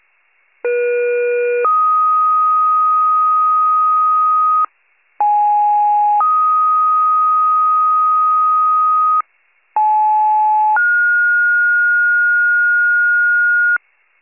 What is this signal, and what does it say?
Two-tone page, heard as a one-second tone, then a three-second tone, about 480 then 1250 Hz, about 830 then 1250 Hz, about 860 then 1430 Hz. It is heard from 0.6 to 13.8 s.